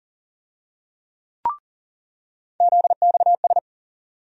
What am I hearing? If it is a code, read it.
Morse code, heard as a keyed tone, about 40 wpm, 700 Hz. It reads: ZXS
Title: USA Call Sign Postfixes - 3 Letters 40wpm